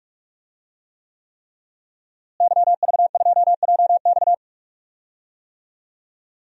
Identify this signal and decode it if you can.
Morse code, heard as a keyed tone, about 46 wpm, 700 Hz. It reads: YV2JX
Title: DXCC-100 Call Signs - 5 Characters - Repeated 3x - 45wpm